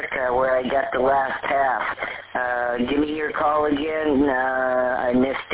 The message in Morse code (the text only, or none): none